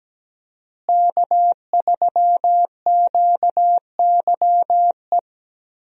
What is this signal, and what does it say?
Morse code, heard as a keyed tone, about 17 wpm, 700 Hz. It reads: K3QYE
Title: USA Call Signs - 5 Characters - Repeated 3x - 17wpm